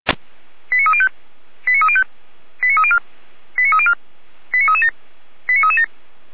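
Five-tone selective call (ZVEI), heard as quick runs of five tones, tones of about 80 ms. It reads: 89305 89305 89304 89304 89337 89337
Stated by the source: Five-tone ZVEI selective call received on an SDR.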